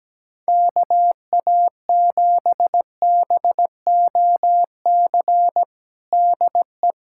Morse code, heard as a keyed tone, about 17 wpm, 700 Hz. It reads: KA7BOC DE
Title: USA Callsign DE Callsign - 6 Characters - 17wpm